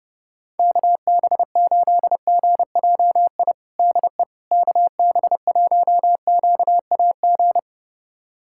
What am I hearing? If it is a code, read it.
Morse code, heard as a keyed tone, about 30 wpm, 700 Hz. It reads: KB8GJI DE KB1QAG
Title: USA Callsign DE Callsign 30wpm